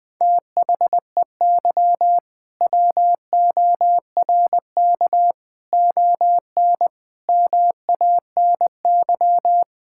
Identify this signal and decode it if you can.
Morse code, heard as a keyed tone, about 20 wpm, 700 Hz. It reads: THEY WORK ON MANY
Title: All-Age Appropriate Jokes - 20wpm